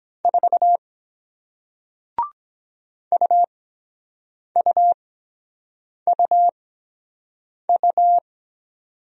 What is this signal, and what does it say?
Morse code, heard as a keyed tone, about 23 wpm, spaced as for 7 wpm, 700 Hz. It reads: 4 UUUU